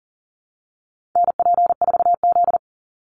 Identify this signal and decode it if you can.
Morse code, heard as a keyed tone, about 40 wpm, 700 Hz. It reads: NP4Z
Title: Call Signs Repeated 2x - 40wpm